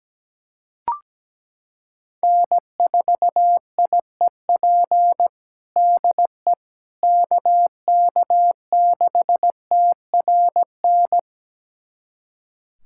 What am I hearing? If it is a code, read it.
Morse code, heard as a keyed tone, about 17 wpm, 700 Hz. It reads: N4IEP DE KK6TRN
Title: USA Callsign DE Callsign 17wpm